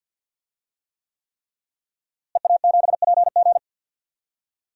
Morse code, heard as a keyed tone, about 47 wpm, 700 Hz. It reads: EA7PG
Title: DXCC-100 Call Signs - 5 Characters - Repeated 3x - 50wpm